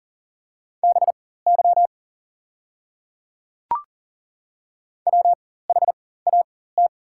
Morse code, heard as a keyed tone, about 40 wpm, spaced as for 19 wpm, 700 Hz. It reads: BY WHAT
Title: 4x Character Spacing - Top 100 Words - 40wpm